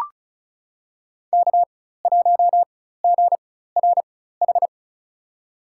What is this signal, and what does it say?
Morse code, heard as a keyed tone, about 36 wpm, spaced as for 16 wpm, 700 Hz. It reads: K1GRH
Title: USA Call Signs - 5 Characters - 4x Character Spacing - 35wpm